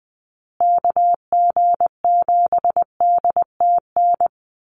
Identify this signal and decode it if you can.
Morse code, heard as a keyed tone, about 20 wpm, 700 Hz. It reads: KG7DTN
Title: USA Call Signs Repeated 2x 20wpm